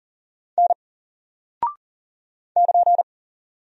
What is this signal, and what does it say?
Morse code, heard as a keyed tone, about 40 wpm, 700 Hz. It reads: N <KN>
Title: Single Character (Letter:Number:Punctuation:Prosign) 40wpm